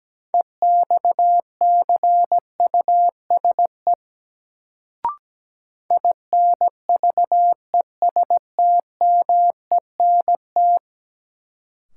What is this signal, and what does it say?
Morse code, heard as a keyed tone, about 17 wpm, 700 Hz. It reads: EXCUSE INVESTMENT
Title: Sentences from Top 700 Words - New Word Drill - 17wpm